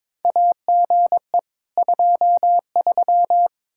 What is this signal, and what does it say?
Morse code, heard as a keyed tone, about 21 wpm, 700 Hz. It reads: AGE 23
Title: Practice QSO - Exchange 3 - More Info - 22wpm